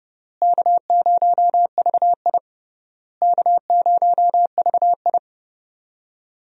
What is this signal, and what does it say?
Morse code, heard as a keyed tone, about 30 wpm, 700 Hz. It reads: K0VI K0VI
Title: USA Call Signs - 4 Characters - Repeated 3x - 30wpm